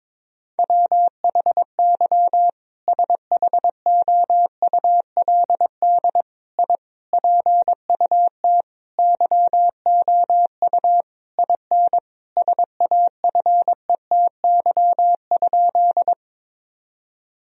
Morse code, heard as a keyed tone, about 22 wpm, 700 Hz. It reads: WHY SHOULD I PUT YOU IN SAFETY?
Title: Sentences from Top 1000 Words - Review Part 4 - 22wpm